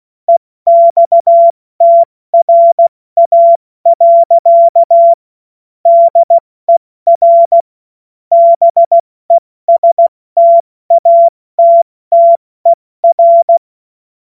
Morse code, heard as a keyed tone, about 16 wpm, 680 Hz. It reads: EXTRA. DER BESTATTER